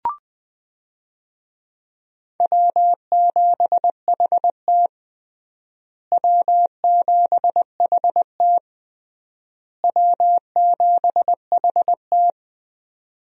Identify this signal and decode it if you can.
Morse code, heard as a keyed tone, about 20 wpm, 700 Hz. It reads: W7HT W7HT W7HT
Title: USA Call Signs - 4 Characters - Repeated 3x - 20wpm